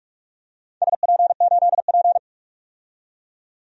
Morse code, heard as a keyed tone, about 47 wpm, 700 Hz. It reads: SP8P
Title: DXCC-100 Call Signs - 4 Characters - 45wpm